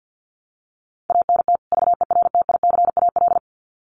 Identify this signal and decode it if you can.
Morse code, heard as a keyed tone, about 50 wpm, 700 Hz. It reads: ANT VERTICAL
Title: Common QSO Phrases 50wpm